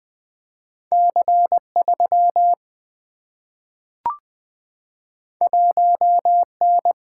Morse code, heard as a keyed tone, about 20 wpm, 700 Hz. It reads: C3 1N
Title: Two Letter-Number Combinations - 20wpm